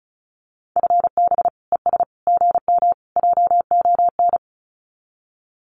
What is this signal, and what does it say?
Morse code, heard as a keyed tone, about 34 wpm, 700 Hz. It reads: FB ES GM JON
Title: Common QSO Phrases Encore 35wpm